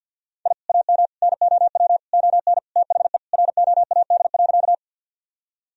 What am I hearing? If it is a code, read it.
Morse code, heard as a keyed tone, about 52 wpm, 700 Hz. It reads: I AM NOW ON THE ROAD.
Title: Sentences from Top 1000 Words - Review Part 2 - 50wpm